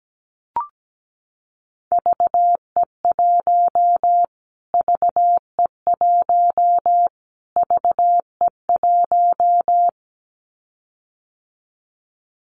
Morse code, heard as a keyed tone, about 17 wpm, 700 Hz. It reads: VE1 VE1 VE1
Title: Callsign Prefixes for Top 25 DXCC - Repeated 3x - 17wpm